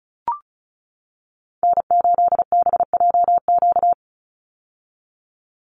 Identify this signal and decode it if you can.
Morse code, heard as a keyed tone, about 34 wpm, 700 Hz. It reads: N8BJQ
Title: Call Signs - 35wpm